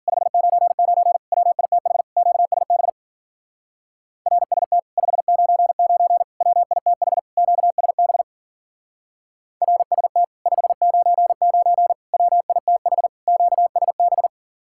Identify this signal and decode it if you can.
Morse code, heard as a keyed tone, about 50 wpm, 700 Hz. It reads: 599 WITH QSB RST 599 WITH QSB RST 599 WITH QSB